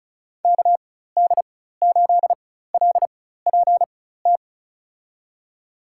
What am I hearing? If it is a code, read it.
Morse code, heard as a keyed tone, about 34 wpm, spaced as for 16 wpm, 700 Hz. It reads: KD8LPT
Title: USA Call Signs - 6 Characters - 4x Character Spacing - 35wpm